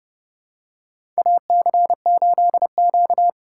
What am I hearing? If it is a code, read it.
Morse code, heard as a keyed tone, about 30 wpm, 700 Hz. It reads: AC8Q
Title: USA Call Signs - 8x Character Spacing - 30wpm